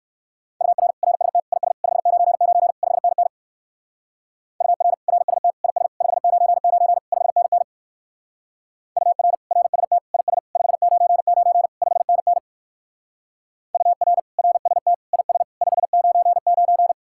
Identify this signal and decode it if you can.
Morse code, heard as a keyed tone, about 64 wpm, 700 Hz. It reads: UR RST IS 599 5NN UR RST IS 599 5NN UR RST IS 599 5NN UR RST IS 599